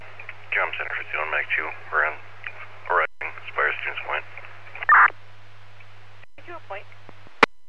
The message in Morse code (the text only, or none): none